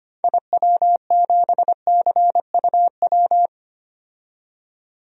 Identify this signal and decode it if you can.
Morse code, heard as a keyed tone, about 25 wpm, 700 Hz. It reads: IW7CUW
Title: DXCC-100 Call Signs - 6 Characters - 25wpm